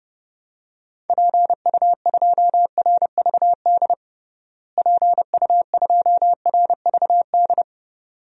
Morse code, heard as a keyed tone, about 30 wpm, 700 Hz. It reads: PU2RVD PU2RVD
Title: DXCC-100 Call Signs - 6 Characters - Repeated 2x - 30wpm